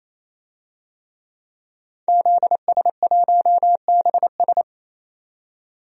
Morse code, heard as a keyed tone, about 27 wpm, 700 Hz. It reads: ZS1BS
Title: DXCC-100 Call Signs - 5 Characters - 28wpm